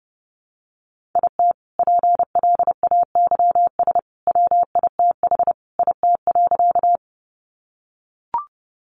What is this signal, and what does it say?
Morse code, heard as a keyed tone, about 30 wpm, 700 Hz. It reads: IT PLAYS WITH IT.
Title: Additional Easy Phrases 30wpm